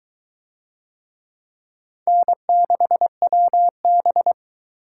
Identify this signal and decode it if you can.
Morse code, heard as a keyed tone, about 22 wpm, 700 Hz. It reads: N6WB